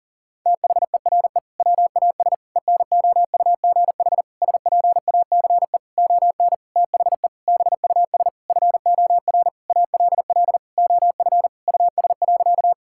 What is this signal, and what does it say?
Morse code, heard as a keyed tone, about 40 wpm, 700 Hz. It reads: THERE WAS ENOUGH SPACE ON THE BUS FOR ALL OF US.
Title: Sentences from Top 1000 Words - Review Part 6 - 40wpm